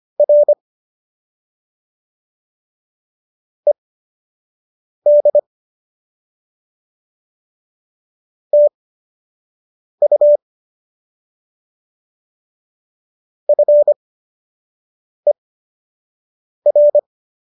Morse code, heard as a keyed tone, about 25 wpm, spaced as for 6 wpm, 590 Hz. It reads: R ED TU FER
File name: qso003 06